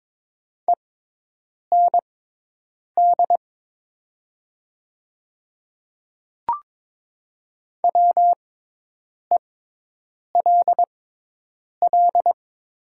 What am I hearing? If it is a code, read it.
Morse code, heard as a keyed tone, about 22 wpm, spaced as for 8 wpm, 700 Hz. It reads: END WELL
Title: 6x Character Spacing - Top 200 Words - 22wpm